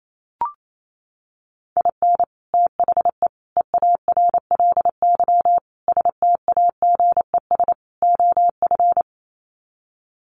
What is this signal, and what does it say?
Morse code, heard as a keyed tone, about 28 wpm, 700 Hz. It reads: IN THE EARLY STAGES OF
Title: Sets of 5 Words Encore 28wpm